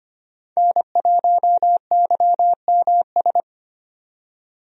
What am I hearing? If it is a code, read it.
Morse code, heard as a keyed tone, about 25 wpm, 700 Hz. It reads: N1YMS
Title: USA Call Signs - 5 Characters - 25wpm